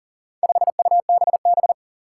Morse code, heard as a keyed tone, about 42 wpm, 700 Hz. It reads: 5UBB
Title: Four Challenging Letter-Number Combinations 40wpm